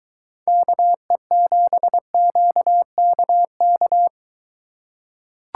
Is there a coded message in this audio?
Yes: Morse code from 0.5 to 4.1 s, heard as a keyed tone, about 23 wpm, 700 Hz. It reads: KE7QKK